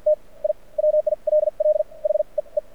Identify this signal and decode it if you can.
Morse code, heard as a keyed tone, about 48 wpm, spaced as for 38 wpm, 610 Hz. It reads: T I WILL HEE
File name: archived / CRSnet